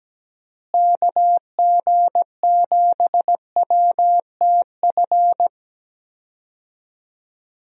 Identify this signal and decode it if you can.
Morse code, heard as a keyed tone, about 17 wpm, 700 Hz. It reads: KG7WTF
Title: USA Call Signs - 6 Characters - Repeated 2x - 17wpm